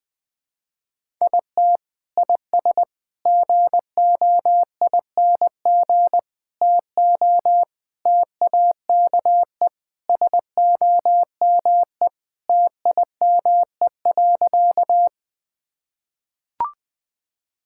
Morse code, heard as a keyed tone, about 20 wpm, 700 Hz. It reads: IT IS GOING TO TAKE SOME TIME.
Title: Sentences with 3-Word Phrases - 20wpm